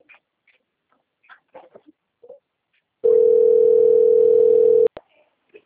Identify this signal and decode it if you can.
Ringback tone (North America), 440 + 480 Hz, with speech.
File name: out- - - - - .95769